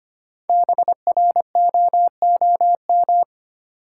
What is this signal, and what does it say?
Morse code, heard as a keyed tone, about 25 wpm, 690 Hz. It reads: BROOM
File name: practice / 25wpm / Easy Words with Hints - 25wpm